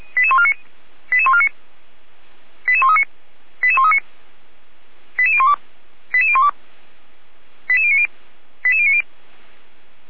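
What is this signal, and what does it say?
Five-tone selective call (ZVEI), heard as quick runs of five tones, tones of about 70 ms. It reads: 88149 88139 88139 88012 88012 88090 88090
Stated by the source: ZVEI five-tone selective call received on an SDR.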